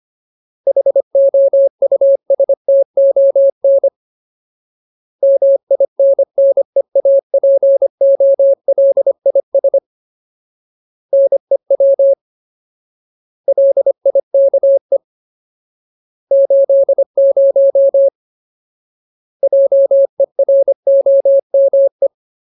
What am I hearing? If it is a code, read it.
Morse code, heard as a keyed tone, about 25 wpm, 550 Hz. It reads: HOUSTON MINNEAPOLIS NEW LIKE 80 JEROME
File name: Ultimate Word List / 25WPM Ultimate